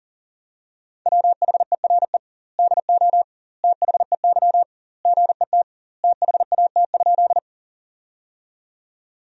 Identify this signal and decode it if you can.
Morse code, heard as a keyed tone, about 40 wpm, 700 Hz. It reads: WHERE DO THEY GET THAT?
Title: 2x Word-Spacing - Sentences from Top 200 Words - 40wpm